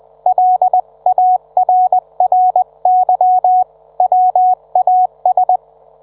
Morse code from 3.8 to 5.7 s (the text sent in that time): WAS